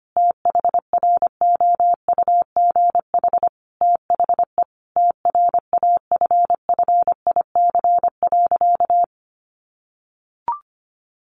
Morse code, heard as a keyed tone, about 25 wpm, 700 Hz. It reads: THROUGH THE TRAFFIC.
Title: Sentences from Top 1300 Words 25wpm